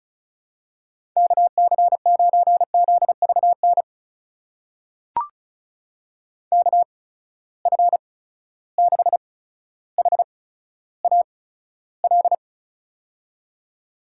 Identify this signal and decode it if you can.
Morse code, heard as a keyed tone, about 36 wpm, spaced as for 15 wpm, 700 Hz. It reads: KC9ZVN KF6HAL
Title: USA Call Signs - 6 Characters - 8x Character Spacing - 35wpm